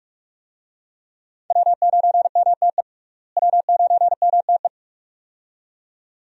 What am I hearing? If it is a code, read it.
Morse code, heard as a keyed tone, about 47 wpm, 700 Hz. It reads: W9MTE W9MTE
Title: USA Call Signs - 5 Characters - Repeated 2x - 45wpm